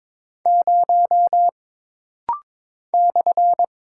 Morse code, heard as a keyed tone, about 22 wpm, 700 Hz. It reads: 0 /
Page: This rapid-fire timing is geared to push your ICR (Instant Character Recognition) skills.